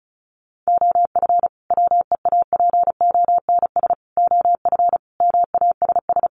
Morse code, heard as a keyed tone, about 34 wpm, 700 Hz. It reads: OF WEAPONS OF MASS
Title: Sets of 5 Words Encore 35wpm